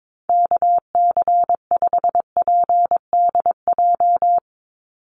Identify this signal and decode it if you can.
Morse code, heard as a keyed tone, about 22 wpm, 700 Hz. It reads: KC5PDJ
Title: USA Call Signs 22wpm